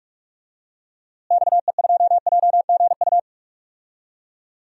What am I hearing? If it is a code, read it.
Morse code, heard as a keyed tone, about 42 wpm, 700 Hz. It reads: XE2JGU
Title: DXCC-100 Call Signs - 6 Characters - Repeated 2x - 45wpm